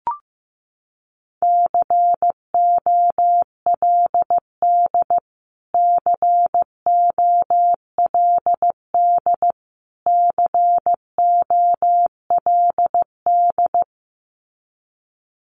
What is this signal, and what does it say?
Morse code, heard as a keyed tone, about 15 wpm, 700 Hz. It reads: COLD COLD COLD
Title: Top 500 Words Repeated 15wpm